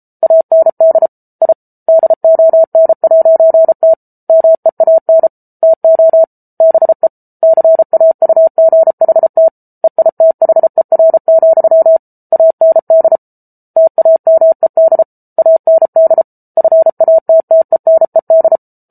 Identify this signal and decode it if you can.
Morse code, heard as a keyed tone, about 34 wpm, 650 Hz. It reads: AND I DON'T MEAN TO BE CAUGHT EITHER, AND TAMED AND FATTENED